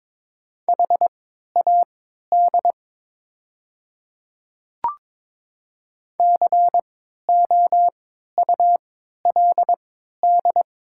Morse code, heard as a keyed tone, about 22 wpm, spaced as for 13 wpm, 700 Hz. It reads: HAD COULD
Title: 3x Character Spacing - Top 500 Words - 22wpm